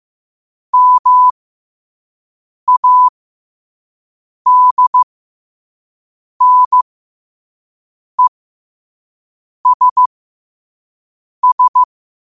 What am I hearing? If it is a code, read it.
Morse code, heard as a keyed tone, about 15 wpm, spaced as for 5 wpm, 1000 Hz. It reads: MADNESS
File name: 5wpm